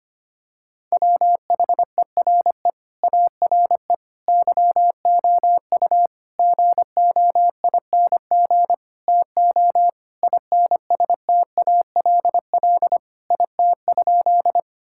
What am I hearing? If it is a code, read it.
Morse code, heard as a keyed tone, about 25 wpm, 700 Hz. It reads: WHERE ARE YOU GOING TO INSTALL IT?
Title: Sentences from Top 1600 Words 25wpm